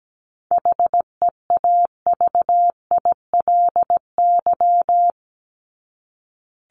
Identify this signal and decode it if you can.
Morse code, heard as a keyed tone, about 17 wpm, 700 Hz. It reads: HEAVILY
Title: LY Suffix - Top 100 Words with Base Words 17wpm